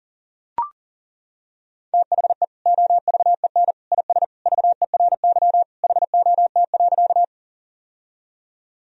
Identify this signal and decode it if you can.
Morse code, heard as a keyed tone, about 40 wpm, 700 Hz. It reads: THE OVEN IS VERY HOT.
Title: Sentences from Top 1000 Words - Review Part 4 - 40wpm